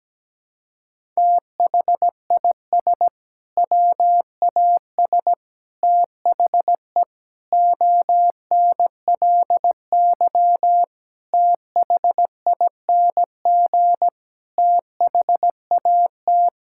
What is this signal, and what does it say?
Morse code, heard as a keyed tone, about 17 wpm, 700 Hz. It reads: THIS WAS THE ONLY THING THAT
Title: Sentences from Top 1000 Words - Review Part 6 - 17wpm